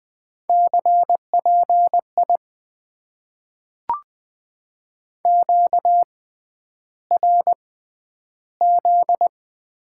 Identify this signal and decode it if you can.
Morse code, heard as a keyed tone, about 20 wpm, spaced as for 10 wpm, 700 Hz. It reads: CPI QRZ
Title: Top 90 QSO Elements - 6x Character Spacing - 20wpm